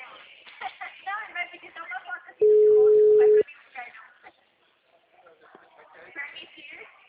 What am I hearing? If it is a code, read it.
Ringback tone (Europe, 425 Hz), with speech.
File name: out- -2005- - - .41973